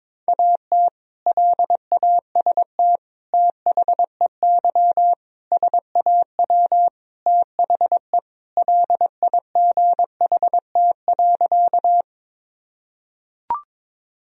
Morse code, heard as a keyed tone, about 22 wpm, 700 Hz. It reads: AT LAST THEY SAW THE LIGHT.
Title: Sentences from Top 1000 Words - Review Part 1 - 22wpm